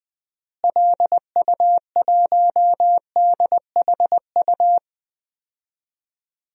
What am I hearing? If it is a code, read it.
Morse code, heard as a keyed tone, about 20 wpm, 700 Hz. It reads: LU1DHU
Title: DXCC-100 Call Signs - 6 Characters - Repeated 2x - 20wpm